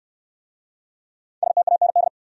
Morse code, heard as a keyed tone, about 66 wpm, 700 Hz. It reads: STAND